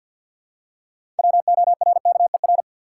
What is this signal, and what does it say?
Morse code, heard as a keyed tone, about 48 wpm, 700 Hz. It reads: WORKER
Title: Six Letter Words - 50wpm